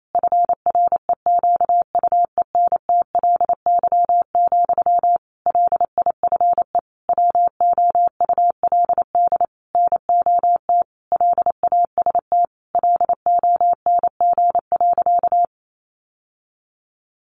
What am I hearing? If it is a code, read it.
Morse code, heard as a keyed tone, about 28 wpm, 700 Hz. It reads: FREQUENTLY, LIFE WOULD NOT LAST LONG.
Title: Sentences from Top 1300 Words 28wpm